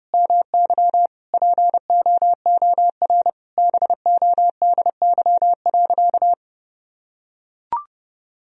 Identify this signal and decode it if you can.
Morse code, heard as a keyed tone, about 32 wpm, 700 Hz. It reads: MY POOR BODY.
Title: Sentences from Top 1000 Words - Review Part 2 - 30wpm